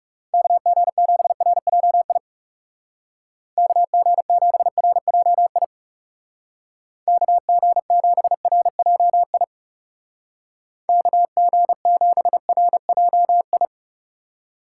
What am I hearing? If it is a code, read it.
Morse code, heard as a keyed tone, about 40 wpm, 700 Hz. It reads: KG7RJI KG7RJI KG7RJI KG7RJI